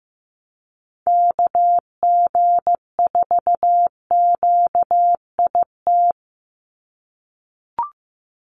Morse code, heard as a keyed tone, about 15 wpm, 700 Hz. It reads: KG4QIT
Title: USA Call Signs 15wpm